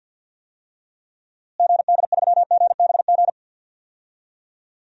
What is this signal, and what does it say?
Morse code, heard as a keyed tone, about 46 wpm, 700 Hz. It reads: GD3GBG
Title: DXCC-100 Call Signs - 6 Characters - Repeated 3x - 50wpm